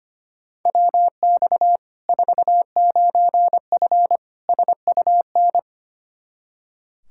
Morse code, heard as a keyed tone, about 25 wpm, 700 Hz. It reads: WX 49F SUN
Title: Practice QSO - Exchange 2 - Info - 25wpm